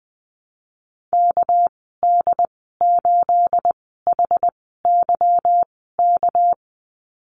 Morse code, heard as a keyed tone, about 20 wpm, spaced as for 15 wpm, 700 Hz. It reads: KD8HYK
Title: 231 Lesson 37g - Letter x - Call Signs 20wpm